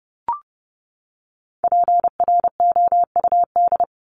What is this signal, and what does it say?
Morse code, heard as a keyed tone, about 30 wpm, 700 Hz. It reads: PROUD
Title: Top 1501-1750 Words 30wpm